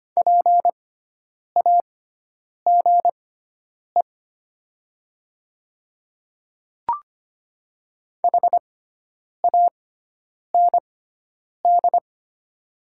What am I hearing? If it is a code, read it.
Morse code, heard as a keyed tone, about 25 wpm, spaced as for 9 wpm, 700 Hz. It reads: PAGE HAND
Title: 6x Character Spacing - Top 300 Words - 25wpm